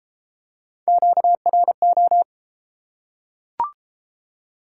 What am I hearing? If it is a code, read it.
Morse code, heard as a keyed tone, about 30 wpm, 700 Hz. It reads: QRO